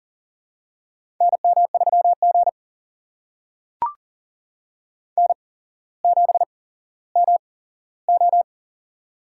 Morse code, heard as a keyed tone, about 40 wpm, spaced as for 17 wpm, 700 Hz. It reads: NM3G N7MO
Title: USA Call Signs - 4 Characters - 8x Character Spacing - 40wpm